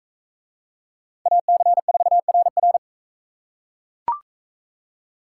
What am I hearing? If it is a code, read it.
Morse code, heard as a keyed tone, about 42 wpm, 700 Hz. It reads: AC4RR